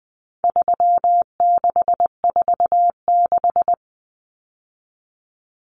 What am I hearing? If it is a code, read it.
Morse code, heard as a keyed tone, about 20 wpm, 700 Hz. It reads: 3646
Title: Common Cut Numbers - 4-Digit 20wpm